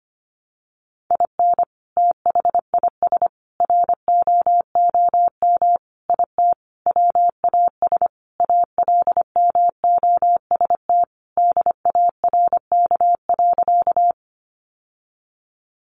Morse code, heard as a keyed tone, about 25 wpm, 700 Hz. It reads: IN THIS ROOM IT WAS ALMOST DARK.
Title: Sentences from Top 700 Words 25wpm